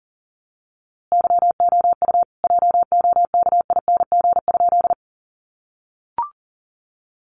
Morse code, heard as a keyed tone, about 38 wpm, 700 Hz. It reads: YOU JOKING?
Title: A Sound Of Thunder 40wpm